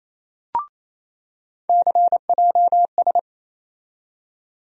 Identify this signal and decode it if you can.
Morse code, heard as a keyed tone, about 28 wpm, 700 Hz. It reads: CJS
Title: USA Call Sign Postfixes - 3 Letters 28wpm